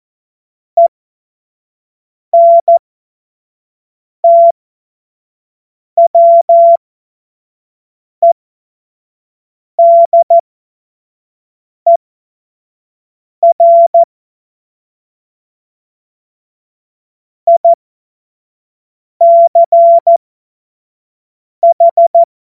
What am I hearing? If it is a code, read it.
Morse code, heard as a keyed tone, about 14 wpm, spaced as for 5 wpm, 680 Hz. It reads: ENTWEDER ICH
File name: Morsekurs / Witze / Witz 18. Juli 2023 25 F70